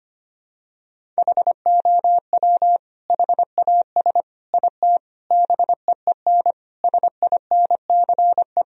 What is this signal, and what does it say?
Morse code, heard as a keyed tone, about 25 wpm, 700 Hz. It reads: HOW HAS IT BEEN SINCE
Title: Sentences from Top 1000 Words - Review Part 1 - 25wpm